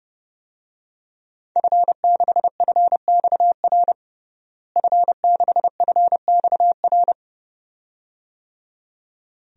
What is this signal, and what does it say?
Morse code, heard as a keyed tone, about 30 wpm, 700 Hz. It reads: F6FXR F6FXR
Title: DXCC-100 Call Signs - 5 Characters - Repeated 2x - 30wpm